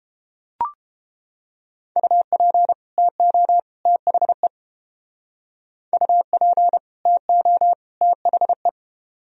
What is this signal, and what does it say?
Morse code, heard as a keyed tone, about 31 wpm, 700 Hz. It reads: UP TO THE UP TO THE